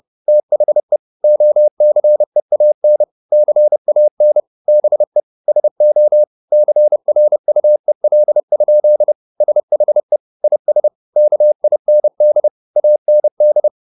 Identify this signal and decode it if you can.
Morse code, heard as a keyed tone, about 30 wpm, 600 Hz. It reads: THE OCEAN CAN BE SO CRUEL? SHE IS KIND AND
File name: the old man and the sea / the old man and the sea 1 30wpm